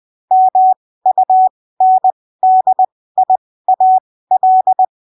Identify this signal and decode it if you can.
Morse code, heard as a keyed tone, about 20 wpm, spaced as for 15 wpm, 750 Hz. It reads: MUNDIAL
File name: arquivos / texto / texto 14 20ppm